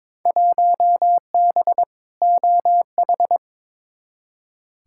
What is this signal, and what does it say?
Morse code, heard as a keyed tone, about 22 wpm, 700 Hz. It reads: 1B OH
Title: Field Day Exchanges - 22wpm